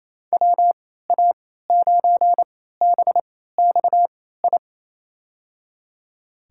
Morse code, heard as a keyed tone, about 28 wpm, spaced as for 16 wpm, 700 Hz. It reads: WA9BXI
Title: USA Call Signs - 6 Characters - 3x Character Spacing - 28wpm